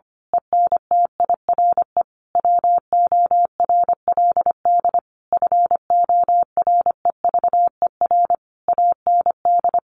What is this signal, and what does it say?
Morse code, heard as a keyed tone, about 25 wpm, 700 Hz. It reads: ENTIRE WORLD FOREVER AND
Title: A Sound Of Thunder 25wpm